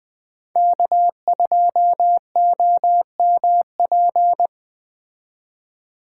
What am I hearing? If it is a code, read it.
Morse code, heard as a keyed tone, about 20 wpm, 700 Hz. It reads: K2OMP
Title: Speed Racing - USA 1x3 Call Signs 17wpm